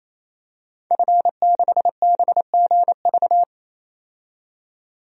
Morse code, heard as a keyed tone, about 28 wpm, 700 Hz. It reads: F6BGV
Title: DXCC-100 Call Signs - 5 Characters - 28wpm